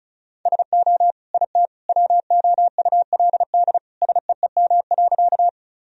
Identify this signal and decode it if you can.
Morse code, heard as a keyed tone, about 36 wpm, 700 Hz. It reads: SO IT WOULD SEEM.
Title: Sentences from Top 1000 Words - Review Part 1 - 35wpm